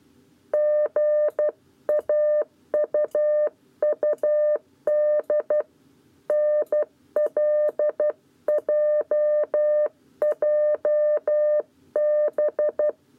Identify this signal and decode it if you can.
Morse code, heard as a keyed tone, about 12 wpm, 600 Hz. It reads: GAUUD NLJJB
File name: 7b lezione cw 1